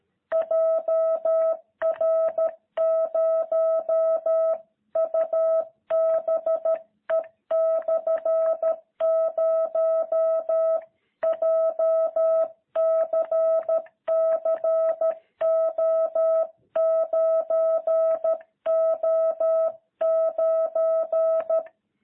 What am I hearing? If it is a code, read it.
Morse code, heard as a keyed tone, about 13 wpm, 660 Hz. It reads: JR0UBE/0 JCCO9O9
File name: jh8cbh / 2021cw